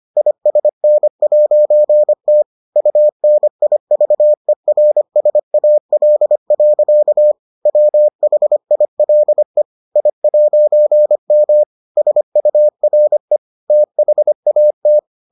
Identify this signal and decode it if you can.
Morse code, heard as a keyed tone, about 25 wpm, 600 Hz. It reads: ISN'T UNIVERSAL. WHILE I'M SURE THAT